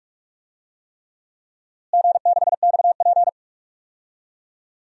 Morse code, heard as a keyed tone, about 44 wpm, 700 Hz. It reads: G6XP
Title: DXCC-100 Call Signs - 4 Characters - Repeated 3x - 45wpm